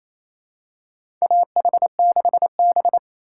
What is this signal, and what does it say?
Morse code, heard as a keyed tone, about 28 wpm, 700 Hz. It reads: AH6B
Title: USA Call Signs - 4 Characters - 3x Character Spacing - 28wpm